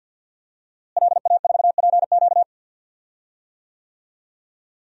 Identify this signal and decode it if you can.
Morse code, heard as a keyed tone, about 52 wpm, 700 Hz. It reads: LA4PQ
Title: DXCC-100 Call Signs - 5 Characters - Repeated 2x - 50wpm